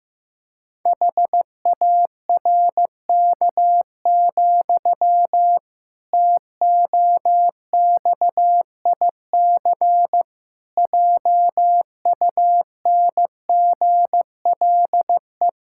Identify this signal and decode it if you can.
Morse code, heard as a keyed tone, about 15 wpm, 700 Hz. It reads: HARK, TOXIC JUNGLE
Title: Pangram Challenge v2 - 15wpm